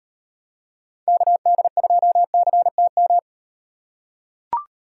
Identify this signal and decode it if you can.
Morse code, heard as a keyed tone, about 35 wpm, 700 Hz. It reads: KD2CTM